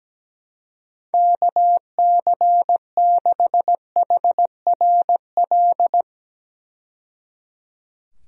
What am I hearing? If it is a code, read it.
Morse code, heard as a keyed tone, about 17 wpm, 700 Hz. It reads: KC6HRL
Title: USA Call Signs - 6 Characters - 17wpm